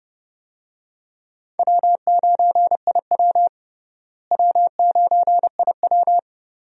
Morse code, heard as a keyed tone, about 30 wpm, 700 Hz. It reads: W9IW W9IW
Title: USA Call Signs - 4 Characters - Repeated 2x - 30wpm